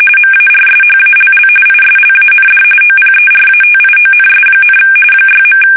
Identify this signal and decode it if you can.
RTTY, about 45.45 baud. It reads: MARSHALL ISLAND OPERATIONS, 6.
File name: fsk850 / battleshipnj